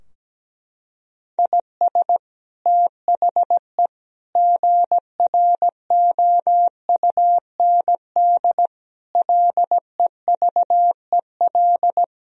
Morse code, heard as a keyed tone, about 17 wpm, 700 Hz. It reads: IS THE GROUND LEVEL
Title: Sentences from Top 1000 Words - Review Part 4 - 17wpm